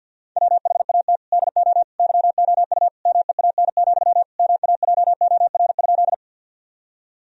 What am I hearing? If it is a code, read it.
Morse code, heard as a keyed tone, about 50 wpm, 700 Hz. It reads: WHAT DO YOU MEAN, MAJOR?
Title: Sentences from Top 1000 Words - Review Part 4 - 50wpm